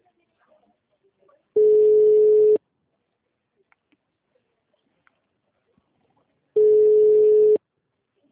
Ringback tone (Europe, 425 Hz).